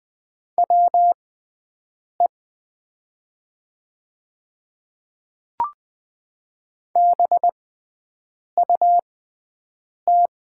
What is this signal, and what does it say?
Morse code, heard as a keyed tone, about 20 wpm, spaced as for 7 wpm, 700 Hz. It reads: WE BUT
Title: 6x Character Spacing - Top 100 Words - 20wpm